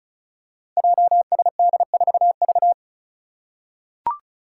Morse code, heard as a keyed tone, about 36 wpm, 700 Hz. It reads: JSD4V
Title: Five Challenging Letter-Number Combinations 35wpm